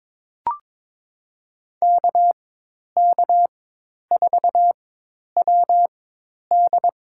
Morse code, heard as a keyed tone, about 21 wpm, spaced as for 10 wpm, 700 Hz. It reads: KK4WD
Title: USA Call Signs - 5 Characters - 4x Character Spacing - 22wpm